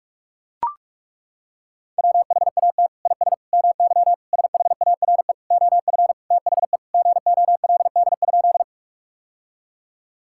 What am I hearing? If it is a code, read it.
Morse code, heard as a keyed tone, about 41 wpm, 700 Hz. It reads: WHAT IS MY SHARE OF THE GOLD?
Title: Sentences from Top 1000 Words - Review Part 6 - 45wpm